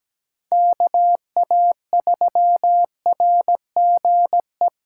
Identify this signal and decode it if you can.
Morse code, heard as a keyed tone, about 17 wpm, 700 Hz. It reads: KA3RGE
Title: USA Call Signs - 2x Character Spacing - 17wpm